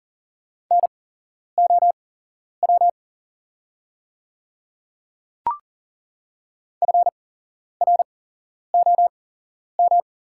Morse code, heard as a keyed tone, about 40 wpm, spaced as for 11 wpm, 690 Hz. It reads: NOW FROM
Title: 8x Character Spacing - Top 100 Words - 40wpm